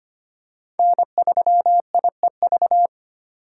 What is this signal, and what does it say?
Morse code, heard as a keyed tone, about 25 wpm, 700 Hz. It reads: N3IEV
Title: USA Call Signs - 5 Characters - 2x Character Spacing - 25wpm